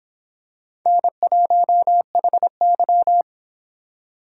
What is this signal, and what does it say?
Morse code, heard as a keyed tone, about 25 wpm, 700 Hz. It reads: N1HY